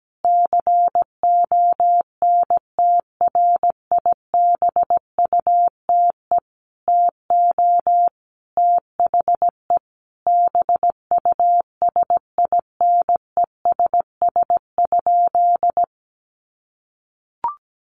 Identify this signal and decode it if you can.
Morse code, heard as a keyed tone, about 17 wpm, 700 Hz. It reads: CONTRIBUTE TO THE BUSINESS?
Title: Sentences from Top 1400 Words 17wpm